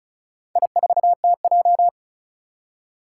Morse code, heard as a keyed tone, about 36 wpm, 690 Hz. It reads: I4TJ
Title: Four Challenging Letter-Number Combinations 35wpm